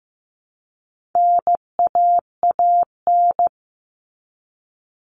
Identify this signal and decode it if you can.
Morse code, heard as a keyed tone, about 15 wpm, 700 Hz. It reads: NAAN
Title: All Cut Numbers - 4-Digit 15wpm